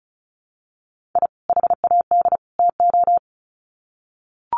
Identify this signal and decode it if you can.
Morse code, heard as a keyed tone, about 34 wpm, 700 Hz. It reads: I HAD TO
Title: Sets of 3 Words 35wpm